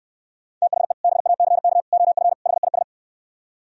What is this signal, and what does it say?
Morse code, heard as a keyed tone, about 68 wpm, 700 Hz. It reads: THE BACK OF HIS